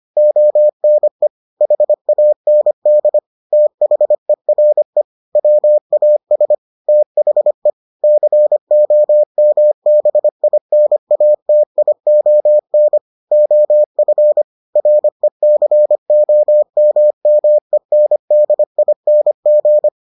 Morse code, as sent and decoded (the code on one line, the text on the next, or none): --- -. / .... .- -. -.. / - .... .-. / .-- .- ... / - .... / -.-. --- -- -... .. -. .- - .. --- -. / --- ..-. / .-. -.-. --- -- -- . -. -.. .. -. --.
ONE HAND THERE WAS THE COMBINATION OF RECOMMENDING